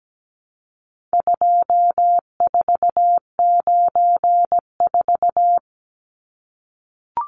Morse code, heard as a keyed tone, about 17 wpm, 700 Hz. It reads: 2494
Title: Numbers - 4-Digit 17wpm